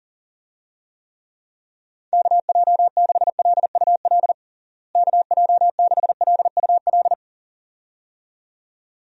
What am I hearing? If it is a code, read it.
Morse code, heard as a keyed tone, about 40 wpm, 700 Hz. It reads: KJ6LUL KJ6LUL
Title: USA Call Signs - 6 Characters - Repeated 2x - 40wpm